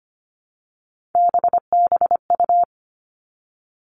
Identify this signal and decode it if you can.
Morse code, heard as a keyed tone, about 25 wpm, 700 Hz. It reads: BBU
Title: Three Challenging Letter-Number Combinations 25wpm